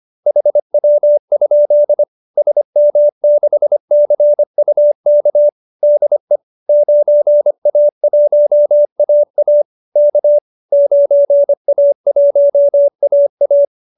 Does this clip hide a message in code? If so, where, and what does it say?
Yes: Morse code from 0.3 to 13.6 s, heard as a keyed tone, about 25 wpm, 590 Hz. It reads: HW? SM6CUK DE 9A1AA K 9A1AA